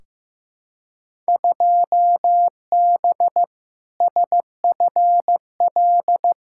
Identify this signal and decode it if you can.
Morse code, heard as a keyed tone, about 15 wpm, 700 Hz. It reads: 2B SFL
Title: Field Day Exchanges Encore - 15wpm